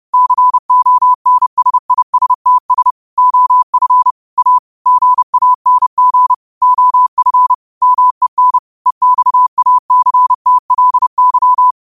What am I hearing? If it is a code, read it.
Morse code, heard as a keyed tone, about 30 wpm, 1000 Hz. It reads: CONSISTS OF A GANG OF MEN EXACTLY
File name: qotdfiles / 30wpm